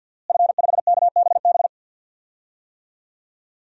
Morse code, heard as a keyed tone, about 48 wpm, 700 Hz. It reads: F5KBB
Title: DXCC-100 Call Signs - 5 Characters - Repeated 3x - 50wpm